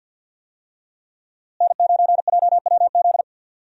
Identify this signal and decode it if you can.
Morse code, heard as a keyed tone, about 52 wpm, 700 Hz. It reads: N9JWZ
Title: USA Call Signs - 5 Characters - Repeated 2x - 50wpm